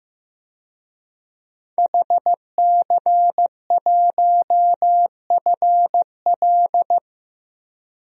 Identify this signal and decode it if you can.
Morse code, heard as a keyed tone, about 15 wpm, 700 Hz. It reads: HC1FL
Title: DXCC-100 Call Signs - 5 Characters - Repeated 2x - 15wpm